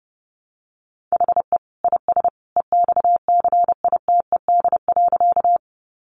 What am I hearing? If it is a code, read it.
Morse code, heard as a keyed tone, about 30 wpm, 700 Hz. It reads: HE IS EXCITED.
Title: Additional Easy Phrases 30wpm